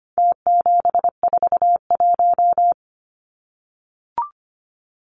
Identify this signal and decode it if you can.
Morse code, heard as a keyed tone, about 25 wpm, 700 Hz. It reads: T741
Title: Numbers - Contest Style - 4-Digit 25wpm